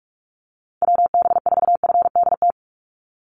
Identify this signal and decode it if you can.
Morse code, heard as a keyed tone, about 44 wpm, 700 Hz. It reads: WB4FDT
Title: Call Signs Repeated 3x - 45wpm